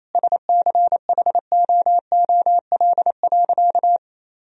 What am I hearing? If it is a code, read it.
Morse code, heard as a keyed tone, about 28 wpm, 700 Hz. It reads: SCHOOL.
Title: Sentences from Top 1000 Words - Review Part 1 - 28wpm